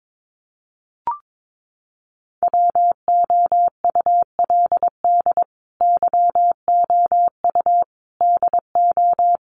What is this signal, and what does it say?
Morse code, heard as a keyed tone, about 22 wpm, 700 Hz. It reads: WOULD YOU DO
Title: Sentences from Top 100 Words 22wpm